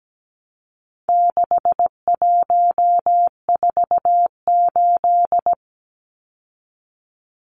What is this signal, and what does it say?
Morse code, heard as a keyed tone, about 17 wpm, 700 Hz. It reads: 6148
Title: Numbers - 4-Digit 17wpm